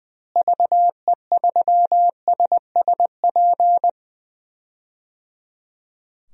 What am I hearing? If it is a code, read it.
Morse code, heard as a keyed tone, about 20 wpm, 700 Hz. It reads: VE3SSP
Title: DXCC-100 Call Signs - 6 Characters - Repeated 3x - 20wpm